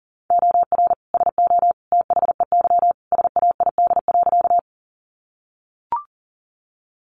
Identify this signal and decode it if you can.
Morse code, heard as a keyed tone, about 40 wpm, 700 Hz. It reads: OR SO THEY SAID.
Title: Sentences from Top 100 Words 40wpm